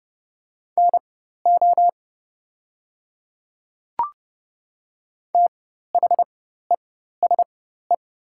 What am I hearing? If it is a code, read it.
Morse code, heard as a keyed tone, about 30 wpm, spaced as for 14 wpm, 700 Hz. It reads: NO THESE
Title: 4x Character Spacing - Top 100 Words - 30wpm